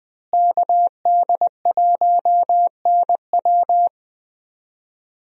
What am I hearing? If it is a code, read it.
Morse code, heard as a keyed tone, about 20 wpm, 700 Hz. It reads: KD1NW
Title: USA Call Signs - 5 Characters - Repeated 3x - 20wpm